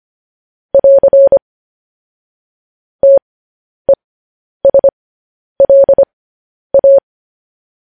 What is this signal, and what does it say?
Morse code, heard as a keyed tone, about 25 wpm, spaced as for 10 wpm, 550 Hz. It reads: <AR> TESLA
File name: podcast / News headlines-1x-25x10